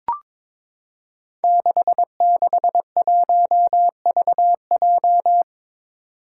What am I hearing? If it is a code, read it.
Morse code, heard as a keyed tone, about 23 wpm, 700 Hz. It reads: 661VJ
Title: Five Challenging Letter-Number Combinations 22wpm